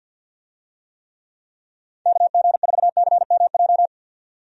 Morse code, heard as a keyed tone, about 48 wpm, 700 Hz. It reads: KG4CMJ
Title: USA Call Signs - 6 Characters - 50wpm